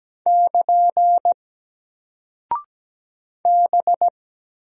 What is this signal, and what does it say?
Morse code, heard as a keyed tone, about 17 wpm, 700 Hz. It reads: <KN> B
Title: Single Character (Letter:Number:Punctuation:Prosign) Alt 17wpm